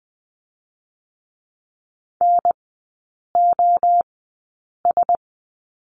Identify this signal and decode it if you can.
Morse code, heard as a keyed tone, about 20 wpm, 700 Hz. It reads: N O S
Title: 114 Lesson 19e - Letter w - Three Characters 20wpm